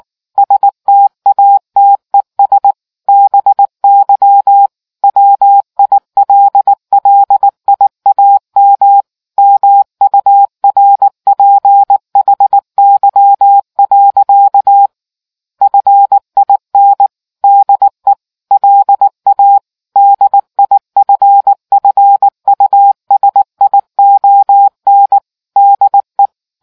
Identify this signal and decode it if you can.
Morse code, heard as a keyed tone, about 19 wpm, 800 Hz. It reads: STATES BY WILLIAM MURPHY. FIN DE LA DIFFUSION DE